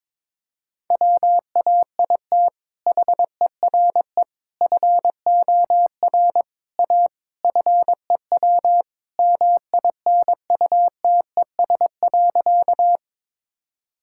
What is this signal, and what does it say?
Morse code, heard as a keyed tone, about 22 wpm, 700 Hz. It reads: WAIT HERE FOR A FEW MINUTES.
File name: practice / 22wpm / Sentences with 3-Word Phrases - 22wpm